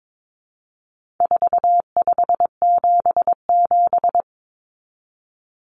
Morse code, heard as a keyed tone, about 22 wpm, 700 Hz. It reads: 4577
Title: Common Cut Numbers - 4-Digit 22wpm